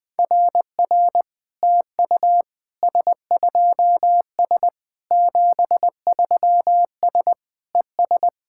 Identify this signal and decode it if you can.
Morse code, heard as a keyed tone, about 21 wpm, 700 Hz. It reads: RR TU S2S 73S ES
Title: SOTA QSO Practice 20wpm